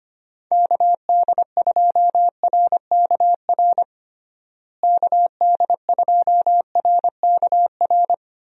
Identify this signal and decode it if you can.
Morse code, heard as a keyed tone, about 25 wpm, 700 Hz. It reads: KD2RKR KD2RKR
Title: USA Call Signs - 6 Characters - Repeated 2x - 25wpm